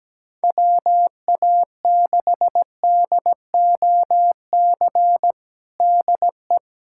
Morse code, heard as a keyed tone, about 17 wpm, 700 Hz. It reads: WA6DOC DE
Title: USA Callsign DE Callsign - 6 Characters - 17wpm